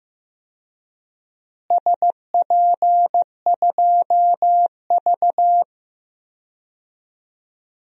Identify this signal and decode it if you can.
Morse code, heard as a keyed tone, about 15 wpm, 700 Hz. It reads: SP2V
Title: DXCC-100 Call Signs - 4 Characters - Repeated 2x - 15wpm